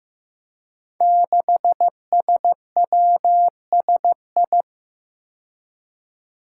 Morse code, heard as a keyed tone, about 15 wpm, 700 Hz. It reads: 6SWSI
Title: Five Challenging Letter-Number Combinations 15wpm